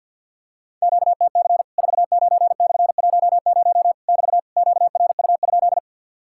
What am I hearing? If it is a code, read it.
Morse code, heard as a keyed tone, about 50 wpm, 700 Hz. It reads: QTC 49/10 <BT> QRU?
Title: Worked All Europe DX Contest - QTC Exchange with Repeats - 50wpm